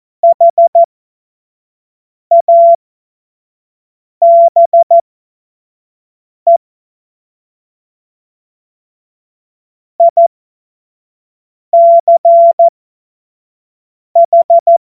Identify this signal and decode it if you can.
Morse code, heard as a keyed tone, about 14 wpm, spaced as for 5 wpm, 680 Hz. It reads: HABE ICH
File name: Morsekurs / Witze / Witz 16 April 2024 25 F70